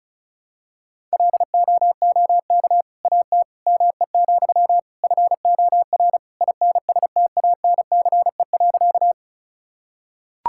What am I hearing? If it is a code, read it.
Morse code, heard as a keyed tone, about 36 wpm, 700 Hz. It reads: LOOK AT ME, FOR INSTANCE.
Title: Sentences from Top 1000 Words - Review Part 5 - 35wpm